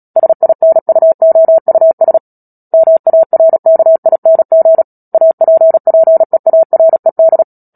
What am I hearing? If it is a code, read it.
Morse code, heard as a keyed tone, about 37 wpm, 650 Hz. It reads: SINUOUS MARKING APPEARED
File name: Book Two / 10-Chapter 10 - Epilogue - 36 WPM